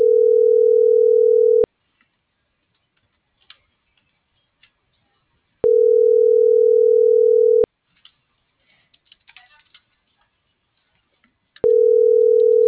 Ringback tone (North America), 440 + 480 Hz.